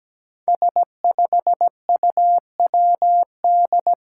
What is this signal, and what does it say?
Morse code, heard as a keyed tone, about 17 wpm, 700 Hz. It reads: S5UWD
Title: Five Challenging Letter-Number Combinations 17wpm